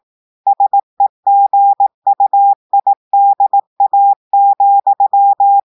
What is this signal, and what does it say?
Morse code, heard as a keyed tone, about 18 wpm, 800 Hz. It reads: SEGUIDA,
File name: portugais texte en clair 2 vitesse 18 mots minutes